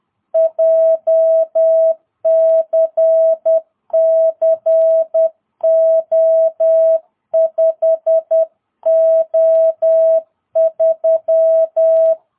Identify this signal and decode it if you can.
Morse code, heard as a keyed tone, about 10 wpm, 660 Hz. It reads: JCC O5O3
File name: jh8cbh / 2022cw